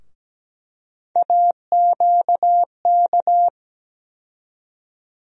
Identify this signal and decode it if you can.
Morse code, heard as a keyed tone, about 17 wpm, 700 Hz. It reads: AQK
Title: USA Call Sign Postfixes - 3 Letters 17wpm